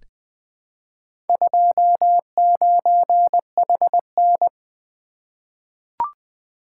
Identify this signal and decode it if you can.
Morse code, heard as a keyed tone, about 20 wpm, 700 Hz. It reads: 29HN
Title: Four Letter-Number Combinations - 20wpm